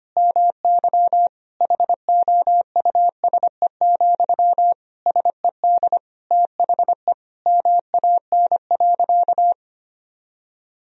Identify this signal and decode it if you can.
Morse code, heard as a keyed tone, about 25 wpm, 700 Hz. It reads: MY HOUSE, SED THE MAN.
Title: Sentences from Top 400 Words and Abbreviations 25wpm